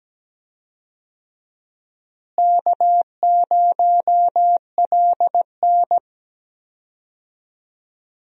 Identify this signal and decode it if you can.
Morse code, heard as a keyed tone, about 17 wpm, 700 Hz. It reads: K0LN
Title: USA Call Signs - 4 Characters - Repeated 3x - 17wpm